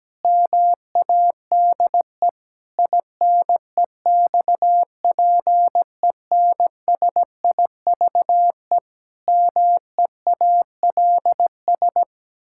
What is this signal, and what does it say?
Morse code, heard as a keyed tone, about 17 wpm, 700 Hz. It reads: MADE INEXPENSIVE MEALS
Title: Pangram Challenge v2 - 17wpm